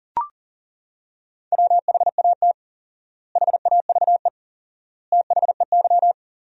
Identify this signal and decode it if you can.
Morse code, heard as a keyed tone, about 40 wpm, 700 Hz. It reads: WHAT HAVE THEY
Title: 4x Word-Spacing - Sentences from Top 500 Words - 40wpm